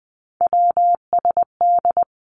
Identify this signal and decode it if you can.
Morse code, heard as a keyed tone, about 20 wpm, 700 Hz. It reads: WSD
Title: Three Challenging Letter-Number Combinations 20wpm